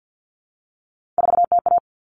Morse code, heard as a keyed tone, about 50 wpm, 700 Hz. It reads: 4TA
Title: Common Cut Numbers - 3-Digit 50wpm